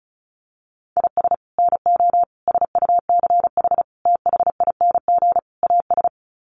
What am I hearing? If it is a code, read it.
Morse code, heard as a keyed tone, about 34 wpm, 700 Hz. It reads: IS NO SUCH THING AS
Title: Sets of 5 Words Encore 35wpm